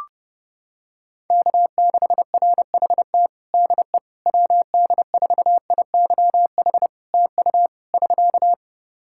Morse code, heard as a keyed tone, about 30 wpm, 700 Hz. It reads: K6RHT DE WD4IYH TU <SK>
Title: Practice QSO - Exchange 4 - Ending - 30wpm